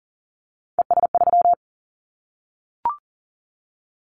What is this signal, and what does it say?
Morse code, heard as a keyed tone, about 43 wpm, 700 Hz. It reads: ES3
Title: Callsign Prefixes for Top 50 DXCC - 40wpm